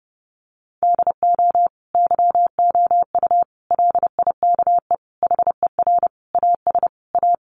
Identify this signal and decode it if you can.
Morse code, heard as a keyed tone, about 30 wpm, 700 Hz. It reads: DO YOU LIKE HER AS A
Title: Sentences from Top 1100 Words 30wpm